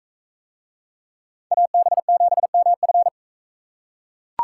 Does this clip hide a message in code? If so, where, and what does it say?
Yes: Morse code from 1.5 to 3.1 s, heard as a keyed tone, about 42 wpm, 700 Hz. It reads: AB7MF